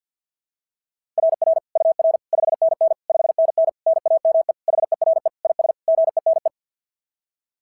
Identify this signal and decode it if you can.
Morse code, heard as a keyed tone, about 52 wpm, 690 Hz. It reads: RR UR 5NN 5NN NAME HERE IS GENE
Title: SOTA QSO Practice 50wpm